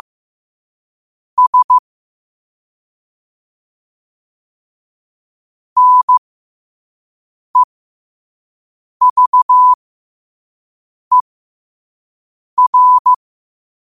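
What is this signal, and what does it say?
Morse code, heard as a keyed tone, about 15 wpm, spaced as for 5 wpm, 1000 Hz. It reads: S NEVER
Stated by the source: Morse code sent at 5 wpm for learning.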